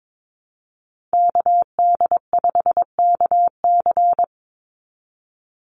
Morse code, heard as a keyed tone, about 22 wpm, 700 Hz. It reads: KD5KC
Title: NA SOTA Chaser Callsigns 22wpm